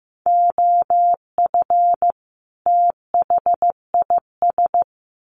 Morse code, heard as a keyed tone, about 14 wpm, 700 Hz. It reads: OF THIS
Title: Sets of 2 Words 15wpm